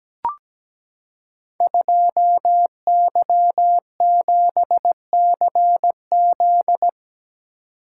Morse code, heard as a keyed tone, about 17 wpm, 700 Hz. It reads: 2Y7CZ
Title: Code Groups - 5 Letter:Numbers - 17wpm